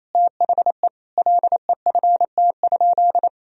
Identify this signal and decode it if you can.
Morse code, heard as a keyed tone, about 28 wpm, 700 Hz. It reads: THE LEFT?
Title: Sentences from Top 1000 Words - Review Part 2 - 28wpm